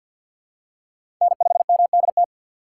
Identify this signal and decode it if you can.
Morse code, heard as a keyed tone, about 47 wpm, 700 Hz. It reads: N5MDT
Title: USA Call Signs - 6x Character Spacing - 50wpm